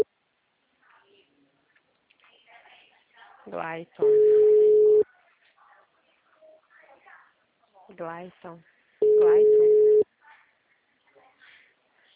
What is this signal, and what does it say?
Ringback tone (Europe, 425 Hz), with speech.
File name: out- -2002- - - .313